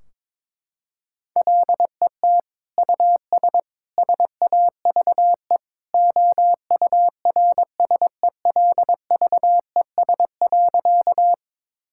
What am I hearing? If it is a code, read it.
Morse code, heard as a keyed tone, about 21 wpm, 700 Hz. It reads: LET US SAVE OURSELVES.
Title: Sentences from Top 1600 Words 22wpm